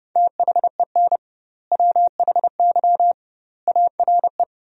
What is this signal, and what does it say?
Morse code, heard as a keyed tone, about 32 wpm, 700 Hz. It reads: THEN WHY ARE
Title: 2x Word-Spacing - Sentences from Top 200 Words - 30wpm